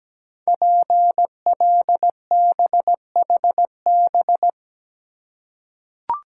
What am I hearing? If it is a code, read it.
Morse code, heard as a keyed tone, about 17 wpm, 700 Hz. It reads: PLBHB
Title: Code Groups - 5 Letter:Numbers - 17wpm